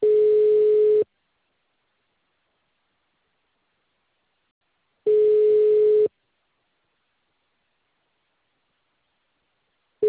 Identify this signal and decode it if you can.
Ringback tone (Europe, 425 Hz).